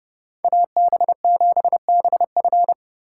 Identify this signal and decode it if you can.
Morse code, heard as a keyed tone, about 29 wpm, 700 Hz. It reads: AB7BF